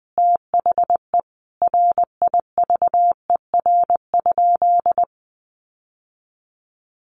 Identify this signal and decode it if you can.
Morse code, heard as a keyed tone, about 20 wpm, 700 Hz. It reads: THE RIVER?
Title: Sentences from Top 1400 Words 20wpm